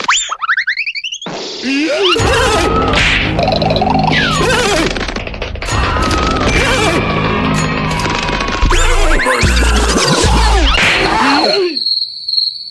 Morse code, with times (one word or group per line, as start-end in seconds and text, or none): none